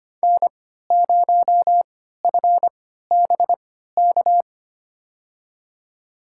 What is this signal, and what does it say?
Morse code, heard as a keyed tone, about 25 wpm, spaced as for 14 wpm, 700 Hz. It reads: N0FBK
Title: USA Call Signs - 5 Characters - 3x Character Spacing - 25wpm